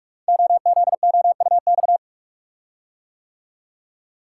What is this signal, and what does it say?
Morse code, heard as a keyed tone, about 44 wpm, 700 Hz. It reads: OZOUX
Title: Code Groups - 5 Letter:Numbers - 45wpm